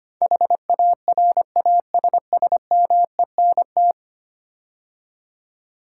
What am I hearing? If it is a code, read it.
Morse code, heard as a keyed tone, about 25 wpm, 700 Hz. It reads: HARASSMENT
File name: 10 Letter Words - First 3 Letters Given - 25wpm